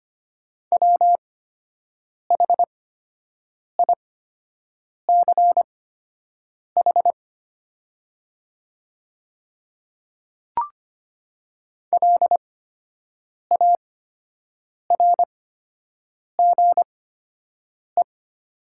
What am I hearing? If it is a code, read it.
Morse code, heard as a keyed tone, about 25 wpm, spaced as for 7 wpm, 700 Hz. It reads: WHICH LARGE
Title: 8x Character Spacing - Top 200 Words - 25wpm